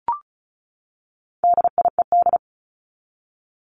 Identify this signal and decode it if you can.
Morse code, heard as a keyed tone, about 34 wpm, 700 Hz. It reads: DIED
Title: ED Suffix - Top 100 Words with Base Words 35wpm